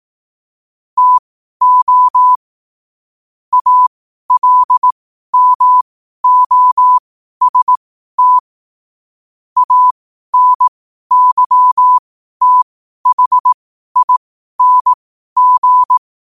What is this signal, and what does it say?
Morse code, heard as a keyed tone, about 18 wpm, spaced as for 12 wpm, 1000 Hz. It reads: TO ALMOST ANYTHING
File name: qotdfiles / 12wpm